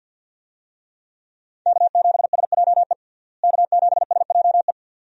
Morse code, heard as a keyed tone, about 45 wpm, 700 Hz. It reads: K7SJE K7SJE
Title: USA Call Signs - 5 Characters - Repeated 2x - 50wpm